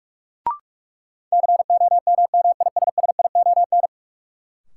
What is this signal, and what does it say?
Morse code, heard as a keyed tone, about 45 wpm, 700 Hz. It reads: COMMISSION
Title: Sentences from Top 1400 Words - New Word Drill Alt - 45wpm